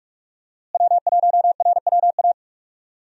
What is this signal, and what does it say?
Morse code, heard as a keyed tone, about 46 wpm, 700 Hz. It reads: W1RWA
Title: USA Call Signs - 5 Characters - Repeated 2x - 45wpm